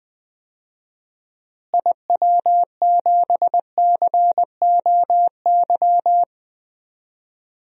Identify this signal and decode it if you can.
Morse code, heard as a keyed tone, about 21 wpm, 700 Hz. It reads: IW7COY
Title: DXCC-100 Call Signs - 6 Characters - Repeated 2x - 20wpm